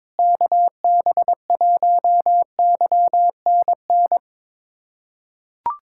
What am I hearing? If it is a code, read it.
Morse code, heard as a keyed tone, about 22 wpm, 700 Hz. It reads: KB1YNN
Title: USA Callsign DE Callsign - 6 Characters - 22wpm